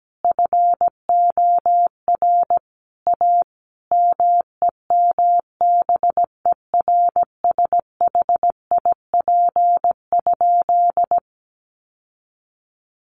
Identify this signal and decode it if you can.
Morse code, heard as a keyed tone, about 17 wpm, 700 Hz. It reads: FOR A MEMBERSHIP?
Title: Sentences from Top 1400 Words 17wpm